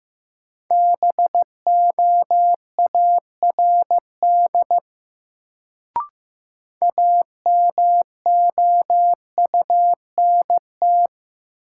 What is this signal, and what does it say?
Morse code, heard as a keyed tone, about 15 wpm, 700 Hz. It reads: BOARD AMOUNT
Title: Sentences from Top 700 Words - New Word Drill - 15wpm